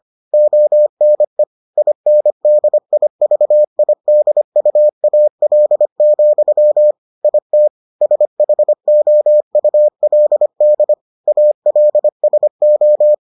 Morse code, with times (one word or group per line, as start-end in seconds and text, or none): 0.3-1.4 ONE
1.8-6.9 INDIVIDUAL,
7.2-7.7 IT
8.0-10.9 SHOULD
11.3-13.1 ALSO